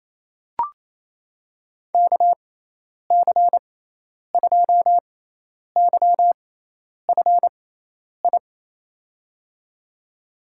Morse code, heard as a keyed tone, about 28 wpm, spaced as for 10 wpm, 700 Hz. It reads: KC2YFI
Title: USA Call Signs - 6 Characters - 6x Character Spacing - 28wpm